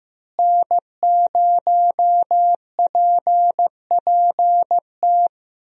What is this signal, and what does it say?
Morse code, heard as a keyed tone, about 15 wpm, 700 Hz. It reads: N0PPT
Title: USA Call Signs - 5 Characters - Repeated 3x - 15wpm